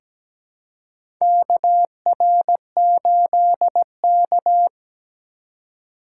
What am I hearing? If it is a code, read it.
Morse code, heard as a keyed tone, about 17 wpm, 700 Hz. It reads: KR8K
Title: USA Call Signs - 4 Characters - 3x Character Spacing - 17wpm